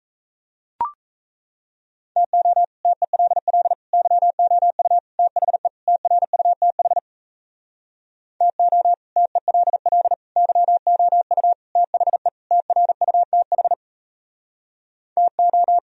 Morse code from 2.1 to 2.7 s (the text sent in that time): TO